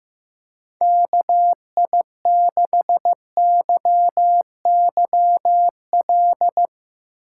Morse code, heard as a keyed tone, about 15 wpm, 700 Hz. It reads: KI6YYL
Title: Speed Racing - USA 2x3 Call Signs 15wpm